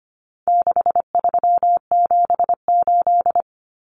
Morse code, heard as a keyed tone, about 25 wpm, 700 Hz. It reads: 6378
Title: Numbers - 4-Digit 25wpm